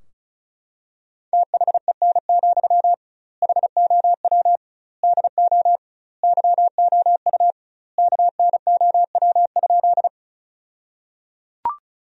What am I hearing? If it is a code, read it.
Morse code, heard as a keyed tone, about 34 wpm, 700 Hz. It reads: THEN, HOW DO YOU KNOW?
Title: 2x Word-Spacing - Sentences from Top 100 Words - 35wpm